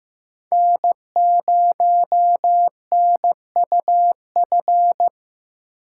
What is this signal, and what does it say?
Morse code, heard as a keyed tone, about 15 wpm, 700 Hz. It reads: N0NUF
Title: USA Call Signs - 4x Character Spacing - 15wpm